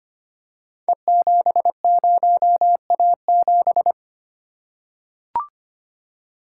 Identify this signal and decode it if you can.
Morse code, heard as a keyed tone, about 25 wpm, 700 Hz. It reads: E70A7
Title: Code Groups - 5 Letter:Numbers - 25wpm